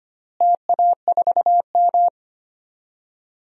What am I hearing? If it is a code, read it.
Morse code, heard as a keyed tone, about 25 wpm, 700 Hz. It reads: TA4M
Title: DXCC-100 Call Signs - 4 Characters - Repeated 3x - 25wpm